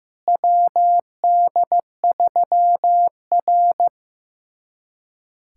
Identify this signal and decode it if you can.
Morse code, heard as a keyed tone, about 15 wpm, 700 Hz. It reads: WD3R
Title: USA Callsign DE Callsign - 4 Characters - 15wpm